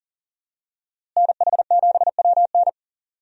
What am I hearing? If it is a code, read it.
Morse code, heard as a keyed tone, about 40 wpm, 700 Hz. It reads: NH7WN
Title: USA Call Signs - 5 Characters - Repeated 2x - 40wpm